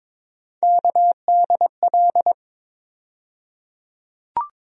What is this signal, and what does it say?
Morse code, heard as a keyed tone, about 22 wpm, 700 Hz. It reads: KDL
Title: USA Call Sign Postfixes - 3 Letters 22wpm